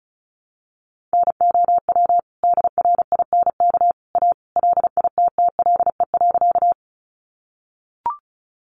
Morse code, heard as a keyed tone, about 34 wpm, 700 Hz. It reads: NOW DRINK A LITTLE.
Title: Sentences from Top 1500 Words 35wpm